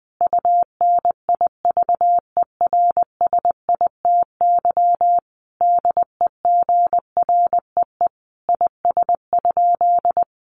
Morse code, heard as a keyed tone, about 20 wpm, 700 Hz. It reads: UNIVERSITY DEGREE IS?
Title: Sentences from Top 1500 Words 20wpm